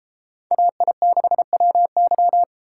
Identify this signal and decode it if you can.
Morse code, heard as a keyed tone, about 30 wpm, 700 Hz. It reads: AI6WY